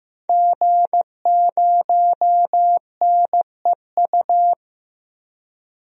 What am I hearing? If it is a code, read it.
Morse code, heard as a keyed tone, about 15 wpm, 700 Hz. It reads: G0NEU
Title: DXCC-100 Call Signs - 5 Characters - 15wpm